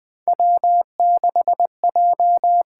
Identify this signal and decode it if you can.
Morse code, heard as a keyed tone, about 20 wpm, 700 Hz. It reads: W6J
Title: USA Call Signs - 8x Character Spacing - 20wpm